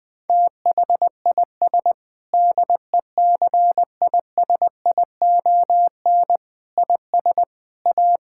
Morse code, heard as a keyed tone, about 20 wpm, 700 Hz. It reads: THIS DECISION IS A
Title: Sentences with 8-Word Phrases - 20wpm